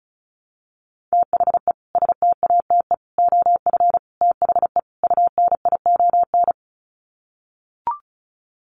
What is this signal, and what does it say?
Morse code, heard as a keyed tone, about 34 wpm, 700 Hz. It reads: THE STATE OF THE UNION
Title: Sets of 5 Words Encore 35wpm